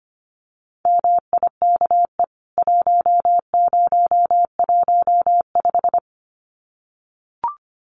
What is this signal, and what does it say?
Morse code, heard as a keyed tone, about 25 wpm, 700 Hz. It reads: MIKE 1015
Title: CWT Exchange 25wpm